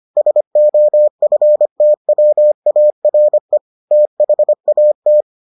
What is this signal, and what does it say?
Morse code, heard as a keyed tone, about 25 wpm, 600 Hz. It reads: SOFTWARE THAT